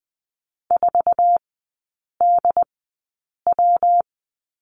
Morse code, heard as a keyed tone, about 20 wpm, spaced as for 8 wpm, 700 Hz. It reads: 4DW
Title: 121 Lesson 20e - Number 3 - Three Characters 20wpm